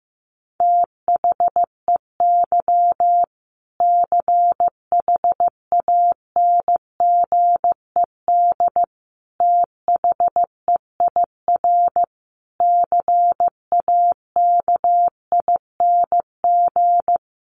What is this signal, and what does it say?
Morse code, heard as a keyed tone, about 15 wpm, 700 Hz. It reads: THEY CHANGED THEIR CAKING
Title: A Sound Of Thunder 15wpm